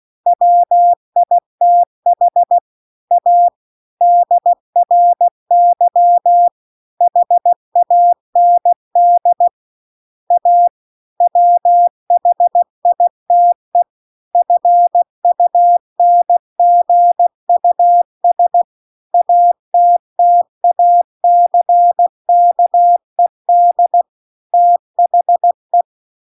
Morse code, heard as a keyed tone, about 16 wpm, 700 Hz. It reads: WITH A DRY HAND A WHITE FUNGUS ATTACKED THE
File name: CW-The-Big-Twoheaded-River-2-16WPM